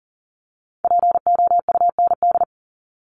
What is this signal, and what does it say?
Morse code, heard as a keyed tone, about 40 wpm, 700 Hz. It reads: POUND
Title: Top 500 Words 40wpm